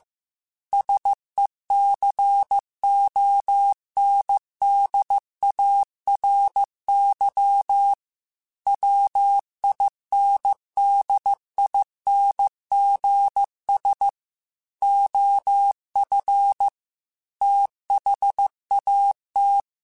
Morse code, heard as a keyed tone, about 15 wpm, 800 Hz. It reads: SECONDARY WINDINGS OF THAT
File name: Technical Subjects / Technical Subjects-mp3s / Sw Pwr Sup @13